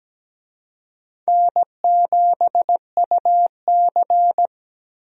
Morse code, heard as a keyed tone, about 17 wpm, 700 Hz. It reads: N7UC
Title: USA Call Signs - 4 Characters - Repeated 2x - 17wpm